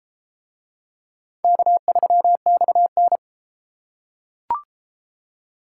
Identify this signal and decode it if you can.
Morse code, heard as a keyed tone, about 33 wpm, 700 Hz. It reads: K3XN